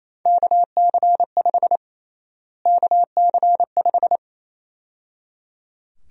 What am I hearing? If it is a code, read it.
Morse code, heard as a keyed tone, about 28 wpm, 700 Hz. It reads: KC5 KC5
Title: USA Call Sign Prefixes - Repeated 2x - 28wpm